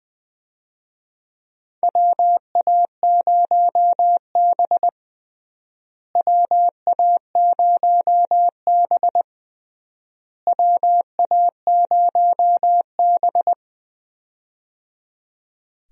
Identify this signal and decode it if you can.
Morse code, heard as a keyed tone, about 20 wpm, 700 Hz. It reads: WA0B WA0B WA0B
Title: USA Call Signs - 4 Characters - Repeated 3x - 20wpm